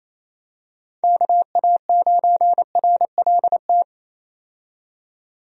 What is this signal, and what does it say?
Morse code, heard as a keyed tone, about 27 wpm, 700 Hz. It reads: KA9RLT
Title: USA Call Signs - 6 Characters - 28wpm